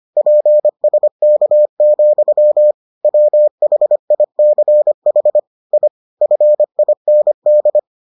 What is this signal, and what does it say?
Morse code, heard as a keyed tone, about 25 wpm, 600 Hz. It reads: PSK, WHICH I FIND